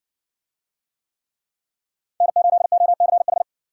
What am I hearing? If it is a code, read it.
Morse code, heard as a keyed tone, about 60 wpm, 700 Hz. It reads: N8OCH